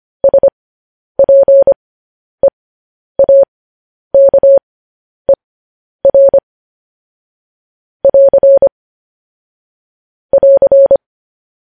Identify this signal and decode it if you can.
Morse code, heard as a keyed tone, about 25 wpm, spaced as for 10 wpm, 550 Hz. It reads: SPEAKER <AR> <AR>